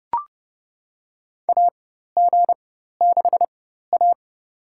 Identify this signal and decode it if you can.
Morse code, heard as a keyed tone, about 29 wpm, spaced as for 14 wpm, 700 Hz. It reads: AG6A
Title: USA Call Signs - 4 Characters - 4x Character Spacing - 30wpm